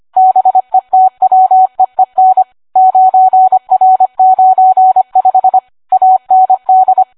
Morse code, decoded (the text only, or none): BETWEEN 9R95 AND